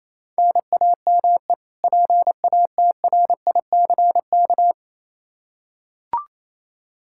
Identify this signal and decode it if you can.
Morse code, heard as a keyed tone, about 28 wpm, 700 Hz. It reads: NAME PATRICK
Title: K1USN SST QSO Practice 28wpm